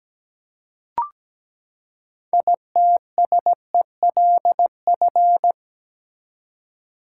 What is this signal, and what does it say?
Morse code, heard as a keyed tone, about 17 wpm, 700 Hz. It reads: ITSELF
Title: Six Letter Words - 17wpm